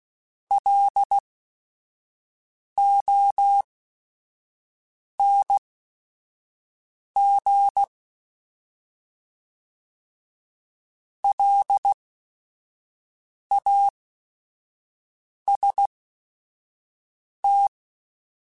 Morse code, heard as a keyed tone, about 16 wpm, spaced as for 5 wpm, 800 Hz. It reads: LONG LAST